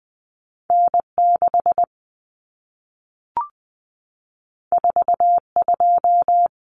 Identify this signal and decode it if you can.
Morse code, heard as a keyed tone, about 20 wpm, 700 Hz. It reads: N6 42
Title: Common Cut Numbers - 2-Digit 20wpm